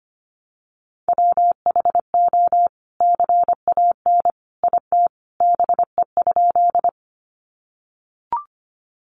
Morse code, heard as a keyed tone, about 25 wpm, 700 Hz. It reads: WHO CAN IT BE?
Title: Sentences from Top 100 Words 25wpm